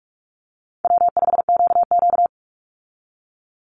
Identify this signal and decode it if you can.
Morse code, heard as a keyed tone, about 44 wpm, 700 Hz. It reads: W5QQ
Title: Easy Call Signs Repeated 3x - 45wpm